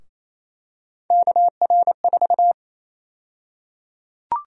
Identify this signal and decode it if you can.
Morse code, heard as a keyed tone, about 28 wpm, 700 Hz. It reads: KR4
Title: USA Call Sign Prefixes - Repeated 2x - 28wpm